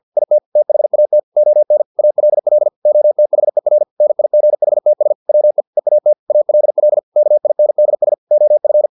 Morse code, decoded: IT THAT ON ALL OTHER NIGHTS WE EAT ALL KINDS OF